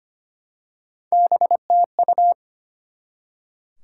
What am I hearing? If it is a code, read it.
Morse code, heard as a keyed tone, about 25 wpm, 700 Hz. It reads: BTU
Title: Top 90 QSO Elements 25wpm